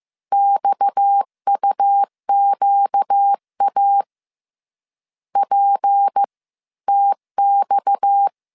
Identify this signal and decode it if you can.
Morse code, heard as a keyed tone, about 15 wpm, 800 Hz. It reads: XUQA P TX